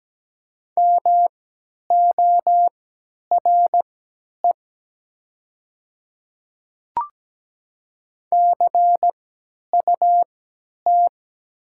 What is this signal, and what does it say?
Morse code, heard as a keyed tone, about 17 wpm, spaced as for 10 wpm, 700 Hz. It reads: MORE CUT
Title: 3x Character Spacing - Top 300 Words - 17wpm